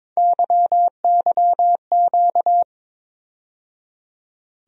Morse code, heard as a keyed tone, about 22 wpm, 700 Hz. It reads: YYQ
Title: USA Call Sign Postfixes - 3 Letters 22wpm